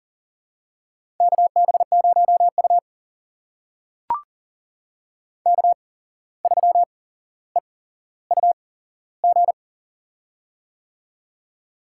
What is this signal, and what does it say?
Morse code, heard as a keyed tone, about 42 wpm, spaced as for 16 wpm, 700 Hz. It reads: KB0U K3EUG
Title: USA Call Signs - 8x Character Spacing - 40wpm